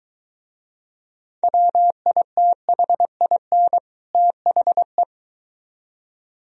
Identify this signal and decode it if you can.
Morse code, heard as a keyed tone, about 23 wpm, 700 Hz. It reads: WITHIN THE